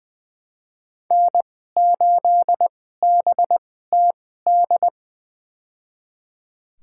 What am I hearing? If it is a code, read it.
Morse code, heard as a keyed tone, about 20 wpm, spaced as for 14 wpm, 700 Hz. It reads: N8BTD
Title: USA Call Signs - 5 Characters - 2x Character Spacing - 20wpm